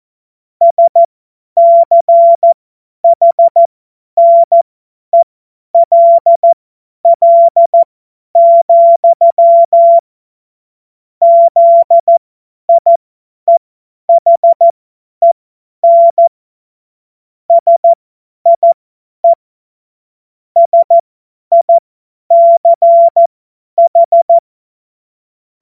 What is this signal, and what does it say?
Morse code, heard as a keyed tone, about 14 wpm, spaced as for 10 wpm, 680 Hz. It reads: SCHNELL, ZIEHEN SIE SICH
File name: Morsekurs / Witze / Witz 19 Dezember 2023 50 F70